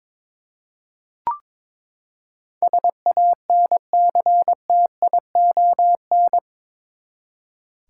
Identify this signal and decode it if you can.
Morse code, heard as a keyed tone, about 22 wpm, 700 Hz. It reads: SANCTION
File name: practice / 22wpm / Eight Letter Words - 22wpm